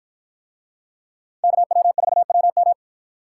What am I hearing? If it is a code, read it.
Morse code, heard as a keyed tone, about 53 wpm, 700 Hz. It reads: KW4WM